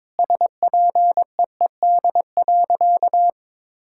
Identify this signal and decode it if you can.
Morse code, heard as a keyed tone, about 22 wpm, 700 Hz. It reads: SPEED.
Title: Sentences from Top 1000 Words - Review Part 5 - 22wpm